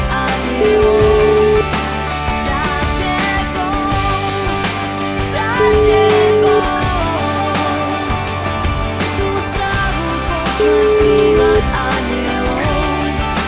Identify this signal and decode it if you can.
Ringback tone (Europe, 425 Hz), with noise.